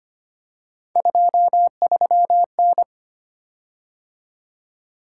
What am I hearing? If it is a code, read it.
Morse code, heard as a keyed tone, about 25 wpm, 700 Hz. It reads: 23N
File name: Three Letter-Number Combinations - 25wpm